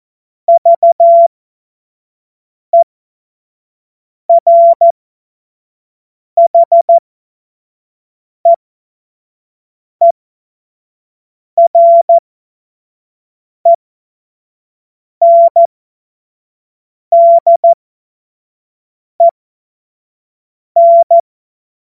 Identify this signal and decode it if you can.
Morse code, heard as a keyed tone, about 14 wpm, spaced as for 5 wpm, 680 Hz. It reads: VERHEERENDEN
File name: Morsekurs / Witze / Witz 02. Februar 2021, 25 F70